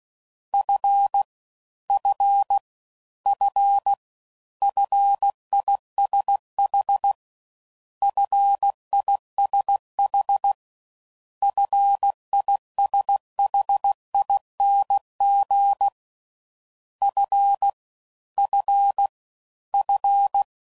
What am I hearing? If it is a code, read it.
Morse code, heard as a keyed tone, about 16 wpm, 800 Hz. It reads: F F F FISH FISH FISHING F F F